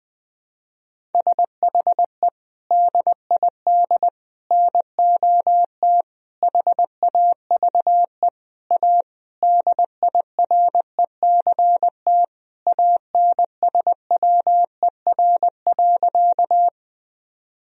Morse code, heard as a keyed tone, about 20 wpm, 700 Hz. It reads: SHE DID NOT HAVE A DIRECT ANSWER.
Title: Sentences from Top 1000 Words - Review Part 2 - 20wpm